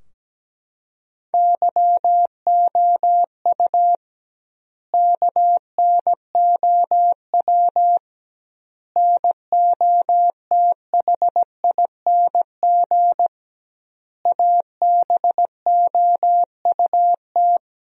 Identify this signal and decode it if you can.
Morse code, heard as a keyed tone, about 17 wpm, 700 Hz. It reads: YOU KNOW NOTHING ABOUT
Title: 2x Word-Spacing - Sentences from Top 500 Words - 17wpm